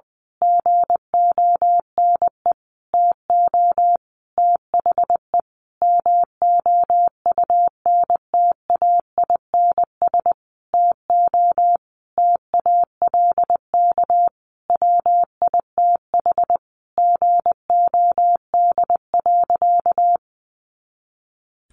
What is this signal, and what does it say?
Morse code, heard as a keyed tone, about 20 wpm, 700 Hz. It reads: GONE TO THE MOUNTAINS TO TALK WITH GOD.
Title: A Sound Of Thunder 20wpm